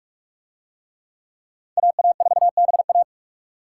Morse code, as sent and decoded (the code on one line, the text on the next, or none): .- .- ....- -... .-
AA4BA